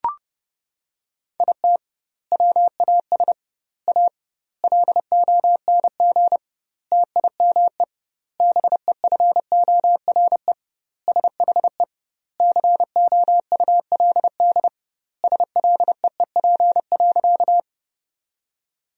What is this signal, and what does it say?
Morse code, heard as a keyed tone, about 30 wpm, 700 Hz. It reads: IT WAS A LONG TIME BEFORE SHE COULD SLEEP.
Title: 2x Word-Spacing - Sentences from Top 500 Words - 30wpm